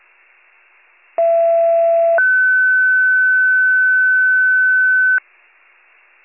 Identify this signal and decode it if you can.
Two-tone page, heard as a one-second tone, then a three-second tone, about 670 then 1500 Hz. It is heard from 1.2 to 5.2 s.